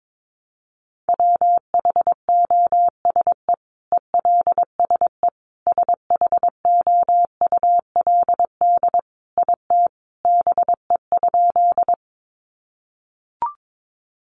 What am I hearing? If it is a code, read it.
Morse code, heard as a keyed tone, about 22 wpm, 700 Hz. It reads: WHOSE ELSE SHOULD IT BE?
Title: Sentences from Top 1200 Words 22wpm